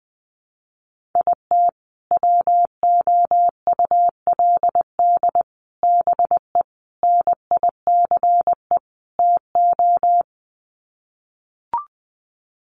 Morse code, heard as a keyed tone, about 20 wpm, 700 Hz. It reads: IT WOULD BE NICE TO
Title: Sets of 5 Words Encore2 20wpm